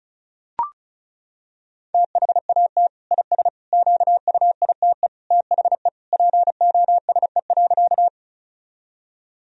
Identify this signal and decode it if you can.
Morse code, heard as a keyed tone, about 34 wpm, 700 Hz. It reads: THAT IS QUITE THE POSE.
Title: Sentences from Top 1000 Words - Review Part 2 - 35wpm